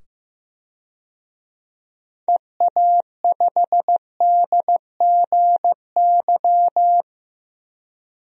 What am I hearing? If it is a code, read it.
Morse code, heard as a keyed tone, about 15 wpm, 700 Hz. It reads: EA5DGY
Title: DXCC-100 Call Signs - 6 Characters - 15wpm